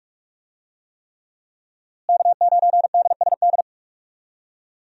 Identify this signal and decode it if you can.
Morse code, heard as a keyed tone, about 46 wpm, 700 Hz. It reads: K9DSD
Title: USA Call Signs - 5 Characters - Repeated 3x - 45wpm